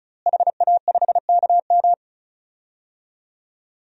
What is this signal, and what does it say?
Morse code, heard as a keyed tone, about 36 wpm, 700 Hz. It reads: HA5KM
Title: DXCC-100 Call Signs - 5 Characters - Repeated 2x - 35wpm